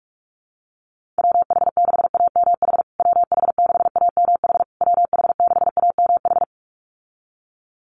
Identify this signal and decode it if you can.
Morse code, heard as a keyed tone, about 44 wpm, 700 Hz. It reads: WH6AMH WH6AMH WH6AMH
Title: USA Call Signs Repeated 3x 45wpm